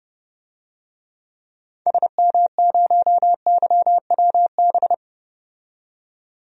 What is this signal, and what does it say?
Morse code, heard as a keyed tone, about 30 wpm, 700 Hz. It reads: SM0YWB
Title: DXCC-100 Call Signs - 6 Characters - 30wpm